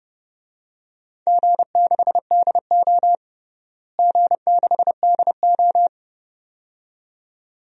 Morse code, heard as a keyed tone, about 30 wpm, 700 Hz. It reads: G6DO G6DO
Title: DXCC-100 Call Signs - 4 Characters - Repeated 2x - 30wpm